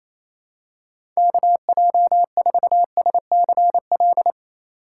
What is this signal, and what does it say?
Morse code, heard as a keyed tone, about 28 wpm, 700 Hz. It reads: KJ4SCL
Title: USA Call Signs - 6 Characters - 2x Character Spacing - 28wpm